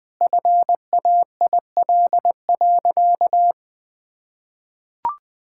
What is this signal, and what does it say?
Morse code, heard as a keyed tone, about 21 wpm, 700 Hz. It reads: FAIL.
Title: Sentences with 7-Word Phrases - 20wpm